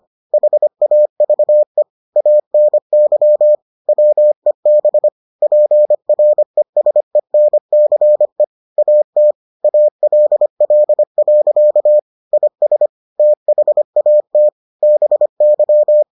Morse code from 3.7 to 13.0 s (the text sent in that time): WEB PRESENCE AT ALL. IS